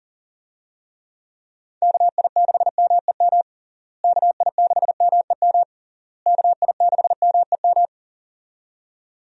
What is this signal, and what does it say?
Morse code, heard as a keyed tone, about 40 wpm, 700 Hz. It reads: KI6MEM KI6MEM KI6MEM
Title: USA Call Signs - 6 Characters - Repeated 3x - 40wpm